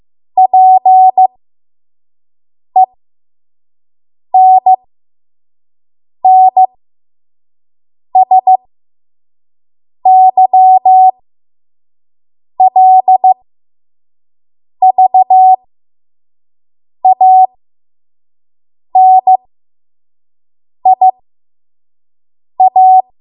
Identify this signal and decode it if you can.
Morse code, heard as a keyed tone, about 15 wpm, spaced as for 5 wpm, 750 Hz. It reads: PENNSYLVANIA
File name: Morse / 05WPM